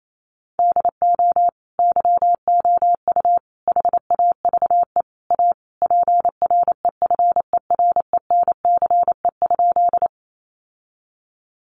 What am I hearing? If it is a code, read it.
Morse code, heard as a keyed tone, about 28 wpm, 700 Hz. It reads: DO YOU HAVE A PREFERENCE?
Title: Sentences from Top 1500 Words 28wpm